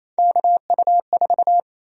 Morse code, heard as a keyed tone, about 27 wpm, 700 Hz. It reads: KU4
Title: USA Call Sign Prefixes - Repeated 2x - 28wpm